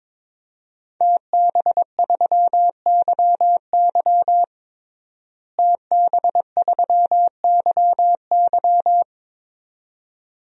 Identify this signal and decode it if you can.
Morse code, heard as a keyed tone, about 21 wpm, 700 Hz. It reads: TB3YY TB3YY
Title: DXCC-100 Call Signs - 5 Characters - Repeated 2x - 22wpm